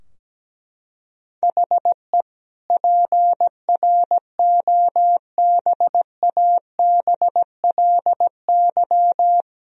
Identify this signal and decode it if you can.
Morse code, heard as a keyed tone, about 17 wpm, 700 Hz. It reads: HE PROBABLY
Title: Sentences from Top 1000 Words - Review Part 4 - 17wpm